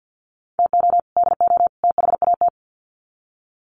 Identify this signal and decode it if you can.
Morse code, heard as a keyed tone, about 48 wpm, 700 Hz. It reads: TO DO THAT
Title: Sets of 3 Words 50wpm